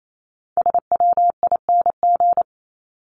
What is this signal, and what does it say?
Morse code, heard as a keyed tone, about 28 wpm, 700 Hz. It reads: SWING
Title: Top 1751-2000 Words 28wpm